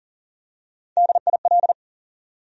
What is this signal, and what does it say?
Morse code, heard as a keyed tone, about 40 wpm, 700 Hz. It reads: DIL
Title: USA Call Sign Postfixes - 3 Letters 40wpm